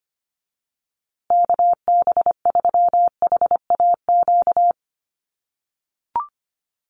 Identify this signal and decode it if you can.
Morse code, heard as a keyed tone, about 25 wpm, 700 Hz. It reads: KB3HAQ
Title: USA Call Signs Repeated 2x 25wpm